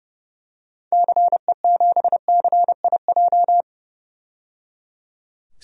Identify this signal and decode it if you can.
Morse code, heard as a keyed tone, about 30 wpm, 700 Hz. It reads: CE7CIJ
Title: DXCC-100 Call Signs - 6 Characters - 30wpm